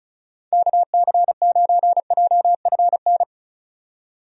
Morse code, heard as a keyed tone, about 35 wpm, 700 Hz. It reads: KC9JFN
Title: USA Call Signs - 6 Characters - 35wpm